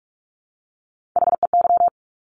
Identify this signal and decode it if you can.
Morse code, heard as a keyed tone, about 46 wpm, 690 Hz. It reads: HEY
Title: Top Three Letter Combinations in Words 45wpm